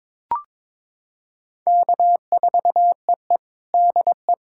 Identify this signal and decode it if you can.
Morse code, heard as a keyed tone, about 22 wpm, 700 Hz. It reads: K4EE DE
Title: USA Callsign DE Callsign - 4 Characters - 22wpm